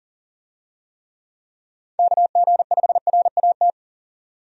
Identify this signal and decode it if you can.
Morse code, heard as a keyed tone, about 40 wpm, 700 Hz. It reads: KG5RAT
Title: USA Call Signs - 6 Characters - 40wpm